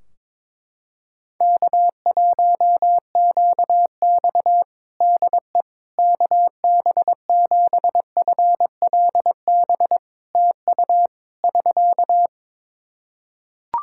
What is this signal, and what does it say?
Morse code, heard as a keyed tone, about 21 wpm, 700 Hz. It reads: K1QX DE KB7FLB TU <SK>
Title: Practice QSO - Exchange 4 - Ending - 22wpm